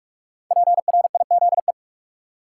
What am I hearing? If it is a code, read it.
Morse code, heard as a keyed tone, about 46 wpm, 700 Hz. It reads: PRIZE
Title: Five Letter Words - 45wpm